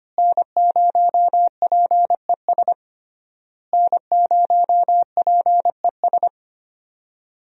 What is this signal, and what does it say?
Morse code, heard as a keyed tone, about 25 wpm, 700 Hz. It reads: N0PES N0PES
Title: USA Call Signs - 5 Characters - Repeated 3x - 25wpm